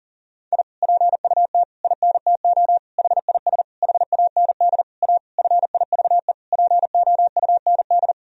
Morse code, heard as a keyed tone, about 40 wpm, 700 Hz. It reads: I PUT INTO HIS HAND A FIVE POUND
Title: Sentences from Top 1000 Words - Review Part 2 - 40wpm